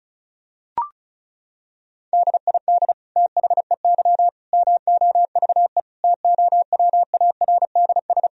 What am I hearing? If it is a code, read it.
Morse code, heard as a keyed tone, about 36 wpm, 700 Hz. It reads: DID THEY MOVE TOWARDS
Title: Sentences from Top 1000 Words - Review Part 2 - 35wpm